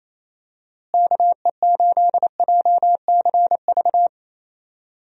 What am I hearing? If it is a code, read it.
Morse code, heard as a keyed tone, about 28 wpm, 700 Hz. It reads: KE8JCV
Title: USA Call Signs - 6 Characters - Repeated 2x - 28wpm